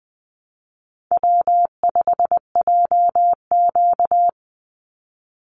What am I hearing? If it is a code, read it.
Morse code, heard as a keyed tone, about 20 wpm, 700 Hz. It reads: W5JQ
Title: Call Signs - 20wpm